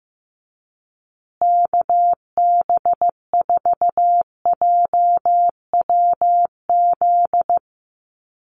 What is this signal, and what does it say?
Morse code, heard as a keyed tone, about 15 wpm, 700 Hz. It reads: KB4JWZ
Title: USA Call Signs Repeated 2x 15wpm